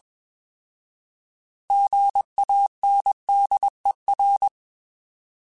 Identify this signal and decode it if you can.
Morse code, heard as a keyed tone, about 21 wpm, 800 Hz. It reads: GANDER
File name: Alphabetical / Alphabetical mp3s / Gs @20